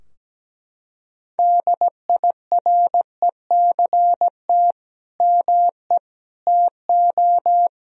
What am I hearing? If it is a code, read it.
Morse code, heard as a keyed tone, about 17 wpm, 700 Hz. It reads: DIRECT ME TO
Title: Sentences from Top 1000 Words - Review Part 2 - 17wpm